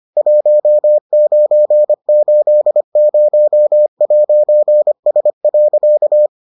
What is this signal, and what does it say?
Morse code, heard as a keyed tone, about 25 wpm, 600 Hz. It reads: 1980'S.